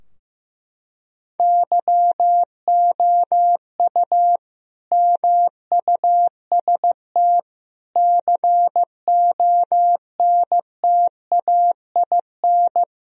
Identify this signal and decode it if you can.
Morse code, heard as a keyed tone, about 15 wpm, 700 Hz. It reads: YOU MUST CONTAIN
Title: Sentences from Top 1000 Words - Review Part 2 - 15wpm